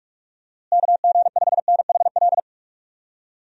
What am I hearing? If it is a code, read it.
Morse code, heard as a keyed tone, about 41 wpm, 700 Hz. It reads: KG5NHL
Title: USA Call Signs - 6 Characters - Repeated 2x - 45wpm